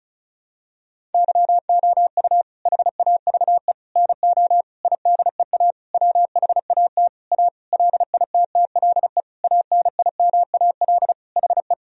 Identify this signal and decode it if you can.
Morse code, heard as a keyed tone, about 36 wpm, 700 Hz. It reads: YOU HAVE NO IDEA WHAT A LITTLE ANIMAL HE
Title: Sentences from Top 1000 Words - Review Part 1 - 35wpm